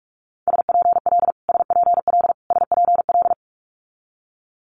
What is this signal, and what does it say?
Morse code, heard as a keyed tone, about 46 wpm, 700 Hz. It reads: SPL SPL SPL
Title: Multi-letter Phonemes Repeated 3x 45wpm